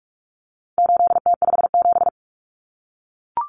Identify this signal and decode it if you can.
Morse code, heard as a keyed tone, about 44 wpm, 700 Hz. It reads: 8T57
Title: Common Cut Numbers - 4-Digit 45wpm